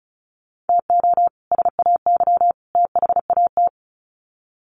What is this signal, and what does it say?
Morse code, heard as a keyed tone, about 34 wpm, 700 Hz. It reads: TO SAY THAT
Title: Sets of 3 Words 35wpm